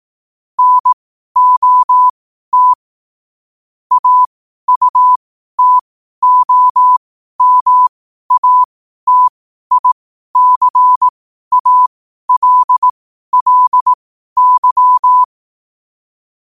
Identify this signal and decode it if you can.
Morse code, heard as a keyed tone, about 18 wpm, spaced as for 12 wpm, 1000 Hz. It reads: NOT AUTOMATICALLY
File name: qotdfiles / 12wpm